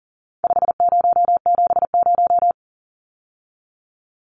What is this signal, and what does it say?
Morse code, heard as a keyed tone, about 40 wpm, 700 Hz. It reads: 5070
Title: Numbers - 4-Digit 40wpm